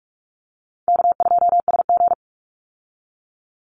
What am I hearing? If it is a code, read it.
Morse code, heard as a keyed tone, about 44 wpm, 700 Hz. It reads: K2SG
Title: Easy Call Signs Repeated 3x - 45wpm